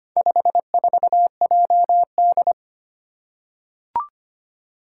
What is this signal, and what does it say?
Morse code, heard as a keyed tone, about 25 wpm, 700 Hz. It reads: 54JD
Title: Four Challenging Letter-Number Combinations 25wpm